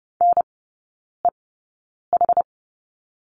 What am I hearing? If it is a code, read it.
Morse code, heard as a keyed tone, about 30 wpm, spaced as for 9 wpm, 700 Hz. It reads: NEH
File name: 081 Lesson 14e - Number 2 - Three Characters 30wpm